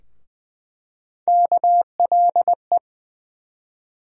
Morse code, heard as a keyed tone, about 20 wpm, 700 Hz. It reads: KLE
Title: USA Call Sign Postfixes - 3 Letters 20wpm